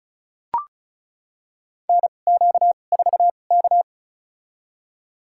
Morse code, heard as a keyed tone, about 36 wpm, spaced as for 25 wpm, 700 Hz. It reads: NQ4K
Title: USA Call Signs - 4 Characters - 2x Character Spacing - 35wpm